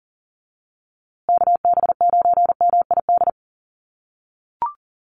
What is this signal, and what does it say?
Morse code, heard as a keyed tone, about 38 wpm, 700 Hz. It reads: KB9MID
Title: USA Call Signs Repeated 2x 40wpm